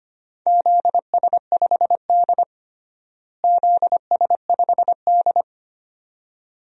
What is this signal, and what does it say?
Morse code, heard as a keyed tone, about 25 wpm, 700 Hz. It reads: ZS5D ZS5D
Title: DXCC-100 Call Signs - 4 Characters - Repeated 2x - 25wpm